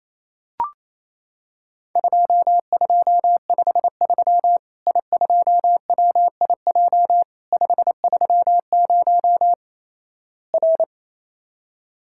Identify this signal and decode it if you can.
Morse code, heard as a keyed tone, about 28 wpm, 700 Hz. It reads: 2253 I2WIJ 530 R
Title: Worked All Europe DX Contest - QTC Exchange - 28wpm